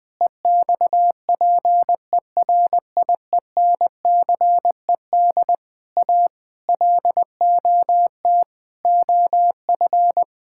Morse code, heard as a keyed tone, about 21 wpm, 700 Hz. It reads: EXPERIENCED A LOT OF
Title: Sentences from Top 1600 Words 20wpm